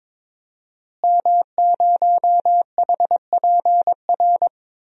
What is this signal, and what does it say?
Morse code, heard as a keyed tone, about 22 wpm, 700 Hz. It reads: M0HPR
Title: DXCC-100 Call Signs - 5 Characters - Repeated 2x - 22wpm